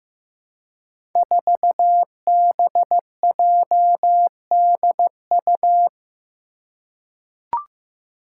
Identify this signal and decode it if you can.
Morse code, heard as a keyed tone, about 15 wpm, 700 Hz. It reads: 4BJDU
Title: Five Challenging Letter-Number Combinations 15wpm